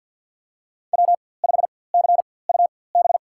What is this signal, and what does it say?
Morse code, heard as a keyed tone, about 51 wpm, spaced as for 23 wpm, 700 Hz. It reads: W5CUB
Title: USA Call Signs - 4x Character Spacing - 50wpm